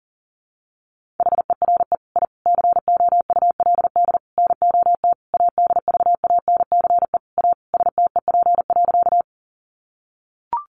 Morse code, heard as a keyed tone, about 40 wpm, 700 Hz. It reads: HERE I COULD NOT ADVANCE A STEP.
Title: Sentences from Top 1400 Words 40wpm